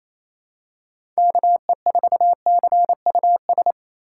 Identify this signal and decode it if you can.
Morse code, heard as a keyed tone, about 28 wpm, 700 Hz. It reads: KE4CUS
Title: USA Call Signs - 6 Characters - 8x Character Spacing - 28wpm